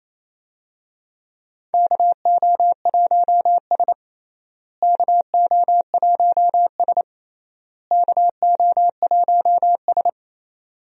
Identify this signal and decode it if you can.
Morse code, heard as a keyed tone, about 28 wpm, 700 Hz. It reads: KO1S KO1S KO1S
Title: USA Call Signs - 4 Characters - Repeated 3x - 28wpm